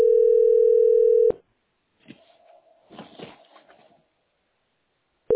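Ringback tone (North America), 440 + 480 Hz.
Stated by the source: Telephone call recorded on the tesspb.mdpbx PBX system.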